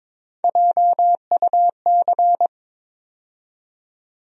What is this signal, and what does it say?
Morse code, heard as a keyed tone, about 22 wpm, 700 Hz. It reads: JUC
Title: USA Call Sign Postfixes - 3 Letters 22wpm